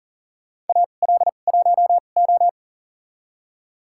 Morse code, heard as a keyed tone, about 40 wpm, spaced as for 29 wpm, 700 Hz. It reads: AL1O
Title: USA Call Signs - 2x Character Spacing - 40wpm